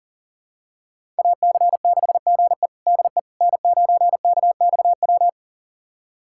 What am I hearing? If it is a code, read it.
Morse code, heard as a keyed tone, about 40 wpm, 700 Hz. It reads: AC6GE DE N9KXW
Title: USA Callsign DE Callsign - 5 Characters - 40wpm